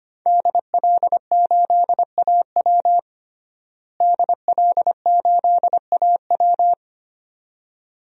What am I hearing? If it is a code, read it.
Morse code, heard as a keyed tone, about 25 wpm, 700 Hz. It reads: DL8AW DL8AW
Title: DXCC-100 Call Signs - 5 Characters - Repeated 2x - 25wpm